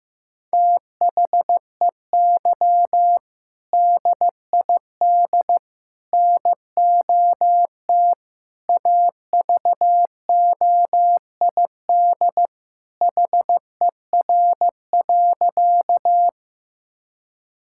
Morse code, heard as a keyed tone, about 15 wpm, 700 Hz. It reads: THEY DID NOT AVOID HER.
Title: Sentences from Top 1000 Words - Review Part 6 - 15wpm